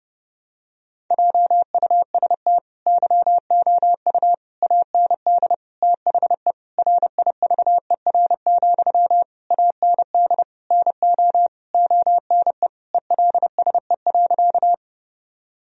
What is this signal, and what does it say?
Morse code, heard as a keyed tone, about 29 wpm, 700 Hz. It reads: JUST YOU AND THE RIVER, AND NO ONE ELSE.
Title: Sentences from Top 1000 Words - Review Part 1 - 30wpm